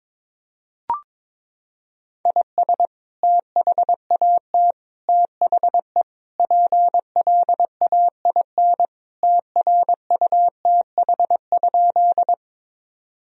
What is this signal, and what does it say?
Morse code, heard as a keyed tone, about 22 wpm, 700 Hz. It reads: IS THAT THE PLAIN TRUTH?
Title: Sentences from Top 1000 Words - Review Part 2 - 22wpm